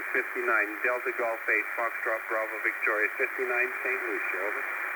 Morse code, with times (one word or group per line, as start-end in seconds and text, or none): none